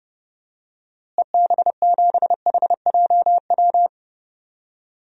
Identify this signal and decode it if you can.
Morse code, heard as a keyed tone, about 30 wpm, 700 Hz. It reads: EB7HJW
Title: DXCC-100 Call Signs - 6 Characters - Repeated 2x - 30wpm